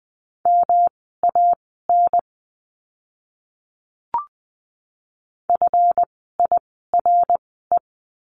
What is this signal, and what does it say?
Morse code, heard as a keyed tone, about 20 wpm, spaced as for 14 wpm, 700 Hz. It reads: MAN FIRE
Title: 139 Lesson 23f - Letter f - Words 20wpm